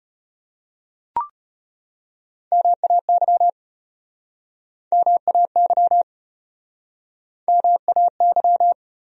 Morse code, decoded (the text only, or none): MAY MAY MAY